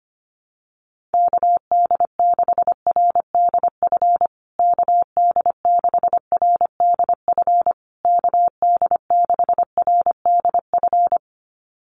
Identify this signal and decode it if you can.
Morse code, heard as a keyed tone, about 25 wpm, 700 Hz. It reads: KD6RDF KD6RDF KD6RDF
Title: USA Call Signs Repeated 3x 25wpm